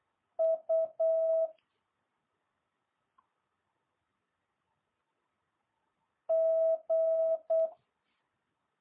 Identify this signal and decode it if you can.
Morse code, heard as a keyed tone, about 8 wpm, 650 Hz. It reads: U G